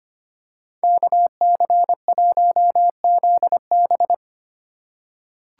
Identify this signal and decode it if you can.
Morse code, heard as a keyed tone, about 25 wpm, 700 Hz. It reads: KC1ZB
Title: USA Call Signs - 2x Character Spacing - 25wpm